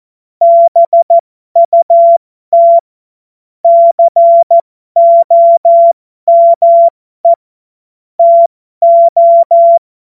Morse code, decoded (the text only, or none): BUT COME TO